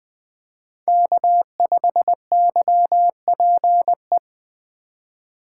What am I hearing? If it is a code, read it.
Morse code, heard as a keyed tone, about 20 wpm, 700 Hz. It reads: K5YPE